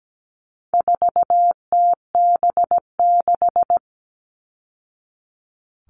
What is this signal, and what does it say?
Morse code, heard as a keyed tone, about 17 wpm, 700 Hz. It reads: 4TB6
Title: All Cut Numbers - 4-Digit 17wpm